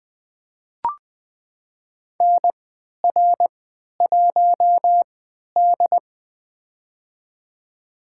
Morse code, heard as a keyed tone, about 20 wpm, spaced as for 11 wpm, 700 Hz. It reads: NR1D
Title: USA Call Signs - 4 Characters - 3x Character Spacing - 20wpm